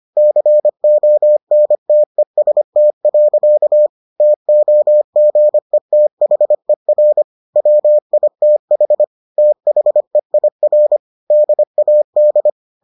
Morse code, heard as a keyed tone, about 25 wpm, 600 Hz. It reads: CONTEST. TOGETHER WITH THEIR DAD